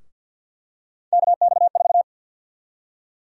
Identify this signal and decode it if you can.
Morse code, heard as a keyed tone, about 52 wpm, 700 Hz. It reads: KX4
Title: USA Call Sign Prefixes - 50wpm